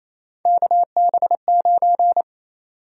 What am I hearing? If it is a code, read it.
Morse code, heard as a keyed tone, about 27 wpm, 700 Hz. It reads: KB9
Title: USA Call Sign Prefixes - Repeated 2x - 28wpm